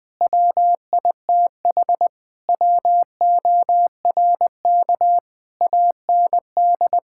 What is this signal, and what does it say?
Morse code, heard as a keyed tone, about 19 wpm, 700 Hz. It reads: WITH WORK AND
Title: Sentences from Top 1000 Words - Review Part 6 - 20wpm